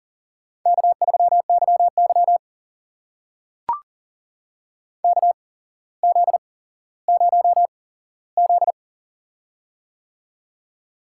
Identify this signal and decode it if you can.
Morse code, heard as a keyed tone, about 40 wpm, spaced as for 17 wpm, 690 Hz. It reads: K3YY KZ0Z
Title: USA Call Signs - 4 Characters - 8x Character Spacing - 40wpm